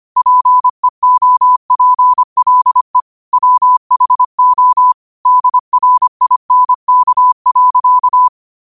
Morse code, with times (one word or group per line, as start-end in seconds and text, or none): 0.2-3.0 PEOPLE
3.3-4.9 WHO
5.3-8.3 DRINK.